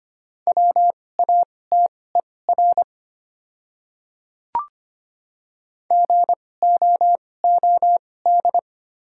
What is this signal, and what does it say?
Morse code, heard as a keyed tone, about 25 wpm, spaced as for 18 wpm, 700 Hz. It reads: WATER GOOD
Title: 2x Character Spacing - Top 200 Words - 25wpm